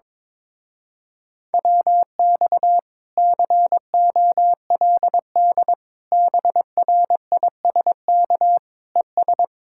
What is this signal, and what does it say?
Morse code, heard as a keyed tone, about 22 wpm, 700 Hz. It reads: WX COLD BRISK ES
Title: Speed-Racing Common QSO Phrases Encore - 22wpm